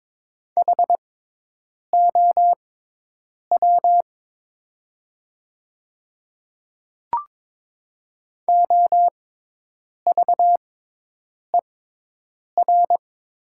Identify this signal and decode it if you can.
Morse code, heard as a keyed tone, about 22 wpm, spaced as for 8 wpm, 700 Hz. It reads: HOW OVER
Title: 6x Character Spacing - Top 100 Words - 22wpm